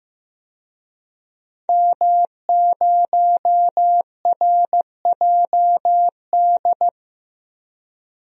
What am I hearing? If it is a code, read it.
Morse code, heard as a keyed tone, about 15 wpm, 700 Hz. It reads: M0RJD
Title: DXCC-100 Call Signs - 5 Characters - Repeated 3x - 15wpm